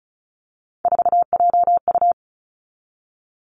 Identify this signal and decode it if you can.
Morse code, heard as a keyed tone, about 36 wpm, 700 Hz. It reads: 4JU
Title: Three Challenging Letter-Number Combinations 35wpm